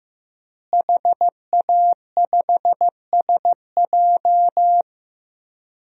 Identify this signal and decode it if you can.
Morse code, heard as a keyed tone, about 15 wpm, 700 Hz. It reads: HA5SJ
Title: DXCC-100 Call Signs - 5 Characters - Repeated 3x - 15wpm